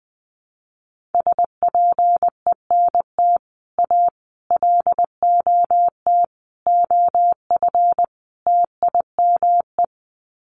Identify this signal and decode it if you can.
Morse code, heard as a keyed tone, about 20 wpm, 700 Hz. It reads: SPENT A LOT OF TIME
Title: Sets of 5 Words 20wpm